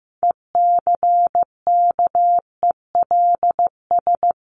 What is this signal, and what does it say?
Morse code, heard as a keyed tone, about 15 wpm, 700 Hz. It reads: ECKELS
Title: A Sound Of Thunder 15wpm